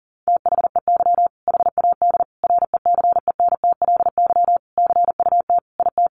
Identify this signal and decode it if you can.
Morse code, heard as a keyed tone, about 40 wpm, 700 Hz. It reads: THEY HAD RECENTLY CUT IT
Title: Sentences from Top 1200 Words 40wpm